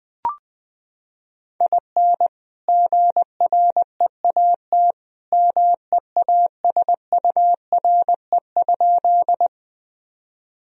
Morse code, heard as a keyed tone, about 21 wpm, 700 Hz. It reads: IN GREAT MEASURE?
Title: Sentences from Top 1000 Words - Review Part 2 - 20wpm